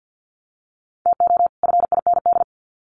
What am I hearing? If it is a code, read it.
Morse code, heard as a keyed tone, about 50 wpm, 700 Hz. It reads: TO FIND
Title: Sets of 2 Words 50wpm